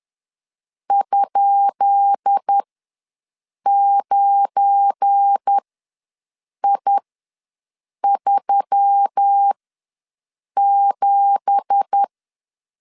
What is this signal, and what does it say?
Morse code, heard as a keyed tone, about 10 wpm, spaced as for 6 wpm, 800 Hz. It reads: ?9I37
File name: LEZIONI / 10 SIGNS / 10B INT ws 10wpm 4134